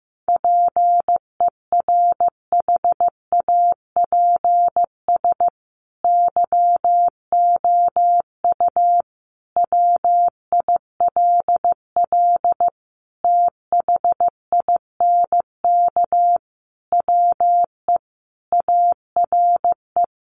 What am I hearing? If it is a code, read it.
Morse code, heard as a keyed tone, about 15 wpm, 700 Hz. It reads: PERHAPS YOU WILL THINK WE ARE
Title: Sentences from Top 1400 Words 15wpm